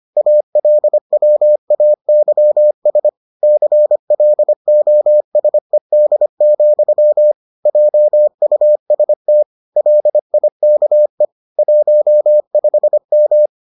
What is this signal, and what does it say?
Morse code, heard as a keyed tone, about 25 wpm, 600 Hz. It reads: ALWAYS CLOSED, JUST LIKE 15M